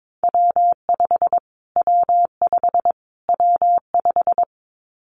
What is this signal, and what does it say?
Morse code, heard as a keyed tone, about 22 wpm, 700 Hz. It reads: W5 W5 W5
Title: Callsign Prefixes for Top 25 DXCC - Repeated 3x - 22wpm